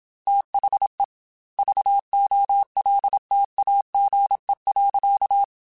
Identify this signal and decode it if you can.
Morse code, heard as a keyed tone, about 27 wpm, 800 Hz. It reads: THE VOLTAGE.
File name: Technical Subjects / Technical Subjects-mp3s / XC @25